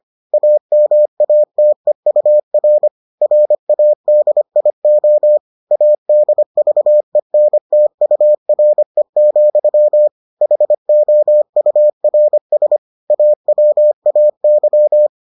Morse code, heard as a keyed tone, about 25 wpm, 600 Hz. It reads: AMATEUR RADIO ADVENTURE, HOURS AWAY